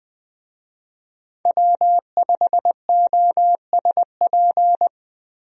Morse code, heard as a keyed tone, about 20 wpm, 700 Hz. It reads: W5OSP
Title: USA Call Signs - 5 Characters - Repeated 3x - 20wpm